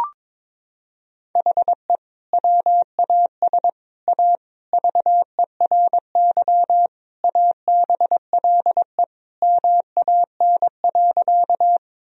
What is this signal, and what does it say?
Morse code, heard as a keyed tone, about 22 wpm, 700 Hz. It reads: HE WAS A VERY ABLE MAN.
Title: Sentences from Top 1000 Words - Review Part 2 - 22wpm